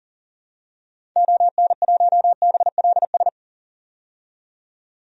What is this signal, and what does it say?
Morse code, heard as a keyed tone, about 40 wpm, 700 Hz. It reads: ON1BLS
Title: DXCC-100 Call Signs - 6 Characters - 40wpm